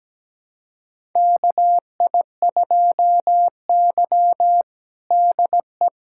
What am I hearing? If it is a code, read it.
Morse code, heard as a keyed tone, about 17 wpm, 700 Hz. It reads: KI2Y DE
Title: USA Callsign DE Callsign - 4 Characters - 17wpm